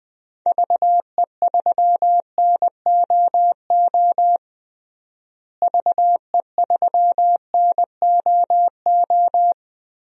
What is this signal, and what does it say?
Morse code, heard as a keyed tone, about 20 wpm, 700 Hz. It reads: VE3NOO VE3NOO
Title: DXCC-100 Call Signs - 6 Characters - Repeated 2x - 20wpm